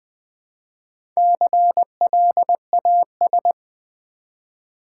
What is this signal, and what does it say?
Morse code, heard as a keyed tone, about 20 wpm, 700 Hz. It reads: CLAS
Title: Build A Word - 1 Letter At A Time - 5-Letter Words - 20wpm